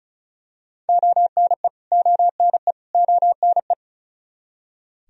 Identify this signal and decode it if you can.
Morse code, heard as a keyed tone, about 34 wpm, 700 Hz. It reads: ONE ONE ONE
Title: Top 100 Words Repeated 35wpm